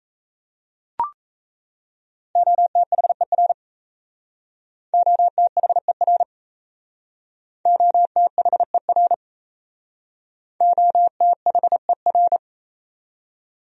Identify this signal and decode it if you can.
Morse code, heard as a keyed tone, about 38 wpm, 700 Hz. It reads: OTHER OTHER OTHER OTHER